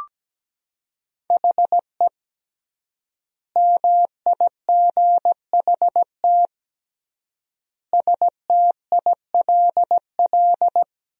Morse code, heard as a keyed tone, about 17 wpm, 700 Hz. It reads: HE MIGHT STILL
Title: 3x Word-Spacing - Sentences from Top 300 Words - 17wpm